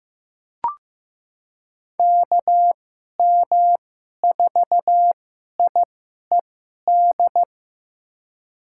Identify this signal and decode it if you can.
Morse code, heard as a keyed tone, about 15 wpm, spaced as for 11 wpm, 700 Hz. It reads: KM4IED
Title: USA Call Signs - 6 Characters - 2x Character Spacing - 15wpm